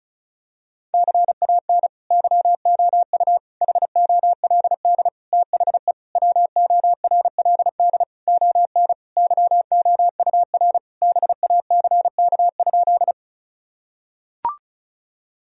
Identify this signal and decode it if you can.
Morse code, heard as a keyed tone, about 34 wpm, 700 Hz. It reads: CAN YOU HOLD THE WORLD ON YOUR BACK?
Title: Sentences from Top 1000 Words - Review Part 2 - 35wpm